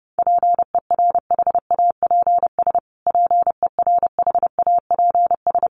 Morse code, heard as a keyed tone, about 30 wpm, 700 Hz. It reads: PERHAPS PERHAPS
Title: Top 500 Words Repeated 30wpm